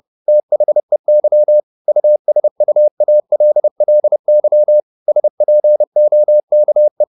Morse code, heard as a keyed tone, about 30 wpm, 600 Hz. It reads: THEY USUALLY SPOKE
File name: the old man and the sea / the old man and the sea 1 30wpm